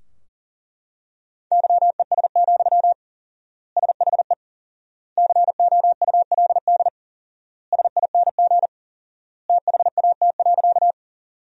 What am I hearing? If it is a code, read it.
Morse code, heard as a keyed tone, about 40 wpm, 700 Hz. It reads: YES, SHE COULD SING THAT.
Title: 4x Word-Spacing - Sentences from Top 500 Words - 40wpm